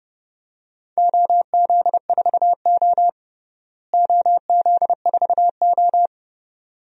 Morse code, heard as a keyed tone, about 31 wpm, 700 Hz. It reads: OZ4O OZ4O
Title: DXCC-100 Call Signs - 4 Characters - Repeated 2x - 30wpm